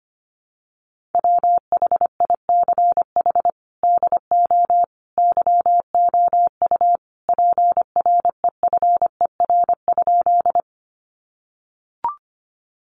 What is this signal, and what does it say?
Morse code, heard as a keyed tone, about 25 wpm, 700 Hz. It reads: WHICH DO YOU PREFER?
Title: Sentences from Top 1100 Words 25wpm